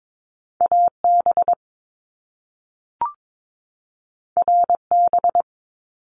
Morse code, heard as a keyed tone, about 22 wpm, 700 Hz. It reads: AB RB
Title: Top Two Letter Combinations in Words 22wpm